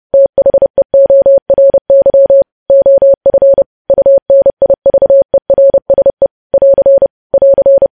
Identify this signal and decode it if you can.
Morse code, heard as a keyed tone, about 30 wpm, 550 Hz. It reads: THEORY OF UNIVERSE <AR> <AR>
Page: news headlines in morse code at 30 wpm